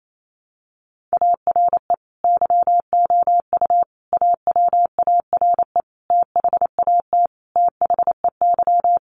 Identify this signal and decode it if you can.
Morse code, heard as a keyed tone, about 28 wpm, 700 Hz. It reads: ARE YOU AWARE THAT THEY
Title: Sentences from Top 1200 Words 28wpm